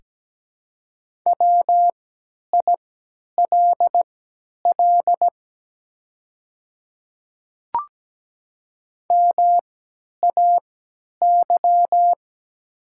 Morse code, heard as a keyed tone, about 17 wpm, spaced as for 10 wpm, 700 Hz. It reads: WILL MAY
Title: 3x Character Spacing - Top 100 Words - 17wpm